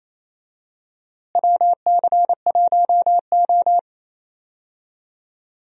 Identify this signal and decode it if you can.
Morse code, heard as a keyed tone, about 28 wpm, 700 Hz. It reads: WC1O
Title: USA Call Signs - 4 Characters - Repeated 2x - 28wpm